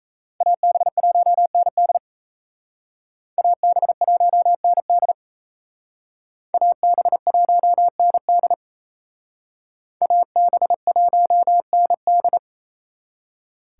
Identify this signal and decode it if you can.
Morse code, heard as a keyed tone, about 35 wpm, 700 Hz. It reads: AB1ND AB1ND AB1ND AB1ND